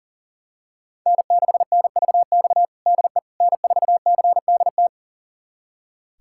Morse code, heard as a keyed tone, about 40 wpm, 700 Hz. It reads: N6NVX DE N4CDT
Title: USA Callsign DE Callsign - 5 Characters - 40wpm